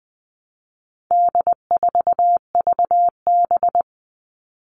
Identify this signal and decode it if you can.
Morse code, heard as a keyed tone, about 20 wpm, 700 Hz. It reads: D4VB
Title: All Cut Numbers - 4-Digit 20wpm